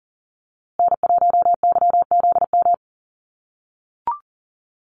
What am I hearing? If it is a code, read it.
Morse code, heard as a keyed tone, about 40 wpm, 700 Hz. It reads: N1YZM
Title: USA Call Signs Repeated 3x 40wpm